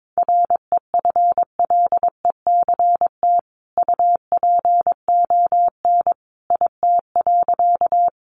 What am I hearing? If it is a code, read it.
Morse code, heard as a keyed tone, about 22 wpm, 700 Hz. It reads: REFLECT UPON IT.
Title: Sentences from Top 1500 Words 22wpm